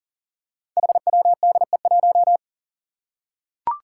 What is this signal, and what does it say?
Morse code, heard as a keyed tone, about 38 wpm, 700 Hz. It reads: HWDE1
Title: Five Challenging Letter-Number Combinations 40wpm